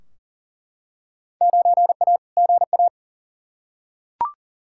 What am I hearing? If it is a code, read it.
Morse code, heard as a keyed tone, about 40 wpm, 700 Hz. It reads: 9A GA
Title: Field Day Exchanges Encore - 40wpm